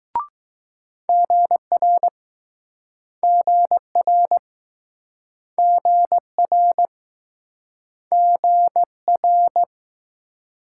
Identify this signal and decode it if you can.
Morse code, heard as a keyed tone, about 21 wpm, 700 Hz. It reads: GR GR GR GR